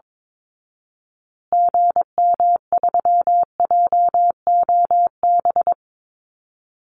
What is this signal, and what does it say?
Morse code, heard as a keyed tone, about 22 wpm, 700 Hz. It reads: GM3JOB
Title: Call Signs Repeated 2x - 22wpm